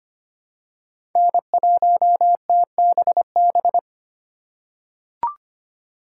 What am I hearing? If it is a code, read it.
Morse code, heard as a keyed tone, about 25 wpm, 700 Hz. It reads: N1TBB
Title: USA Call Signs - 5 Characters - 3x Character Spacing - 25wpm